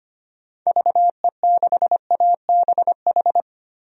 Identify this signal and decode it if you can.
Morse code, heard as a keyed tone, about 25 wpm, 700 Hz. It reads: VE6ABH
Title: DXCC-100 Call Signs - 6 Characters - 25wpm